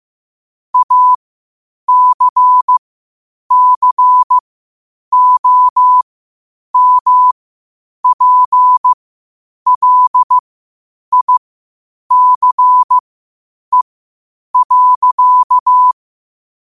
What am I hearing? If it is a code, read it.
Morse code, heard as a keyed tone, about 15 wpm, spaced as for 8 wpm, 1000 Hz. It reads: ACCOMPLICE.
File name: qotdfiles / 8wpm